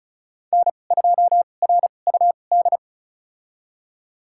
Morse code, heard as a keyed tone, about 36 wpm, spaced as for 25 wpm, 700 Hz. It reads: N2RUD
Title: USA Call Signs - 5 Characters - 2x Character Spacing - 35wpm